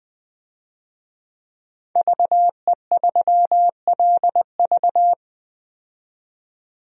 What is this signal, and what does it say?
Morse code, heard as a keyed tone, about 20 wpm, 700 Hz. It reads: VE3LV
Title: DXCC-100 Call Signs - 5 Characters - 20wpm